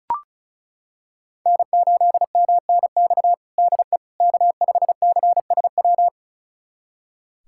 Morse code, heard as a keyed tone, about 36 wpm, 690 Hz. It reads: N8MNX DE K5CSW
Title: USA Callsign DE Callsign 35wpm